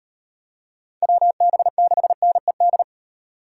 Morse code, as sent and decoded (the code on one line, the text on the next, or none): .-- -... -.... -. -..
WB6NED